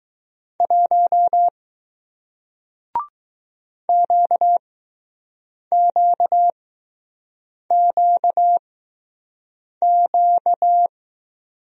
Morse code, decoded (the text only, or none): 1 QQQQ